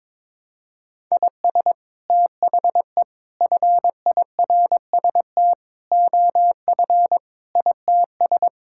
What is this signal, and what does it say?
Morse code, heard as a keyed tone, about 21 wpm, 700 Hz. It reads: IS THE FIRST OF ITS
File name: practice / 22wpm / Sets of 8 Words - 22wpm